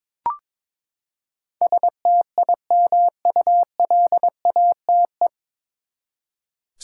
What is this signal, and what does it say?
Morse code, heard as a keyed tone, about 22 wpm, 700 Hz. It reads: STIMULATE
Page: Morse Code Ninja practice 22wpm